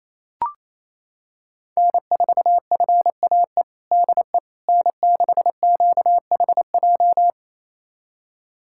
Morse code, heard as a keyed tone, about 27 wpm, 700 Hz. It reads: N4FAE DE N6QHJ
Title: USA Callsign DE Callsign - 5 Characters - 28wpm